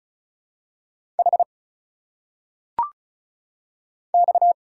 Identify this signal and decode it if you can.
Morse code, heard as a keyed tone, about 36 wpm, 700 Hz. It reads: H X
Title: Single Letters - Phonetic Pronunciation 35wpm